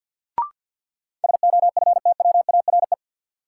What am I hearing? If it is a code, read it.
Morse code, heard as a keyed tone, about 47 wpm, 700 Hz. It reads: SOFTWARE
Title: Sentences from Top 1000 Words - Review Part 3 - New Word Drill - 50wpm